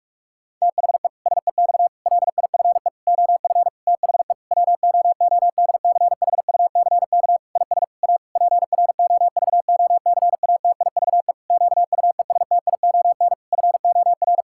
Morse code, heard as a keyed tone, about 41 wpm, 700 Hz. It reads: THE SEX LIFE OF THE WOODCHUCK IS A PROVOCATIVE QUESTION FOR
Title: Pangram Challenge v2 - 45wpm